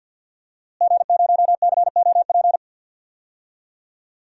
Morse code, heard as a keyed tone, about 52 wpm, 700 Hz. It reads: G0COP
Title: DXCC-100 Call Signs - 5 Characters - Repeated 3x - 50wpm